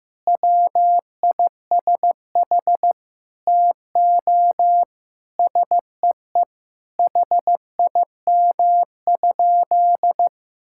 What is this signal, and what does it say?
Morse code, heard as a keyed tone, about 15 wpm, 700 Hz. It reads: WISH TO SEE HIM?
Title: Sentences from Top 1600 Words 15wpm